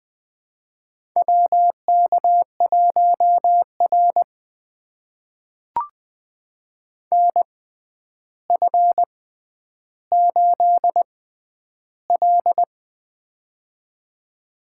Morse code, heard as a keyed tone, about 20 wpm, spaced as for 10 wpm, 700 Hz. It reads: WK1R NF8L
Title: USA Call Signs - 4 Characters - 6x Character Spacing - 20wpm